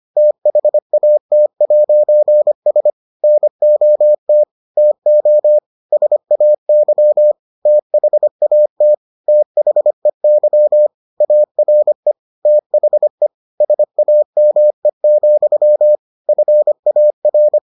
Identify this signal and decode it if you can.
Morse code, heard as a keyed tone, about 25 wpm, 600 Hz. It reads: THAT'S NOT TO SAY THAT THEY ARE THE SAME, FAR